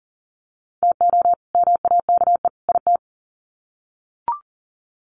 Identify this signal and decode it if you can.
Morse code, heard as a keyed tone, about 40 wpm, 700 Hz. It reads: TO MAKE IT
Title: Sets of 3 Words 40wpm